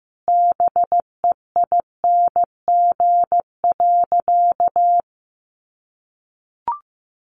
Morse code, heard as a keyed tone, about 15 wpm, 700 Hz. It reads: BEING.
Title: Sentences from Top 1500 Words 15wpm